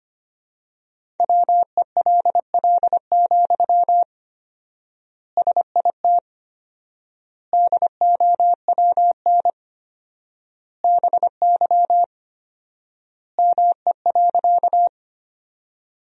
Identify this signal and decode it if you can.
Morse code, heard as a keyed tone, about 25 wpm, 700 Hz. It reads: WELL, SIT DOWN BY ME.
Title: 4x Word-Spacing - Sentences from Top 500 Words - 25wpm